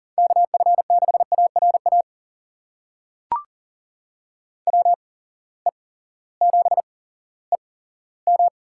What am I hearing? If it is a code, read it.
Morse code, heard as a keyed tone, about 39 wpm, spaced as for 18 wpm, 700 Hz. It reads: KF6ARA WE7EM
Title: USA Call Signs - 8x Character Spacing - 40wpm